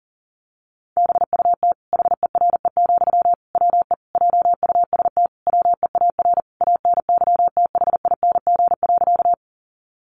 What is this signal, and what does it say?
Morse code, heard as a keyed tone, about 40 wpm, 700 Hz. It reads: BUT HERE, WE JUST WEAR ANYTHING.
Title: Sentences from Top 1500 Words 40wpm